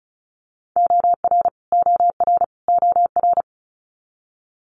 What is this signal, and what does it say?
Morse code, heard as a keyed tone, about 36 wpm, 700 Hz. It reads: OR OR OR
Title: Multi-letter Phonemes Repeated 3x 35wpm